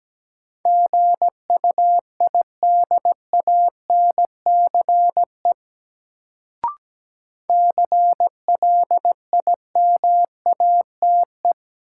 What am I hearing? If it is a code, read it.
Morse code, heard as a keyed tone, about 17 wpm, 700 Hz. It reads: GUIDANCE CLIMATE
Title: Sentences from Top 1600 Words - New Word Drill Alt - 17wpm